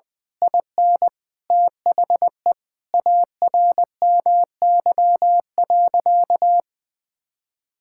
Morse code, heard as a keyed tone, about 20 wpm, 700 Hz. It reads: IN THE ARMY.
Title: Sentences from Top 1000 Words - Review Part 6 - 20wpm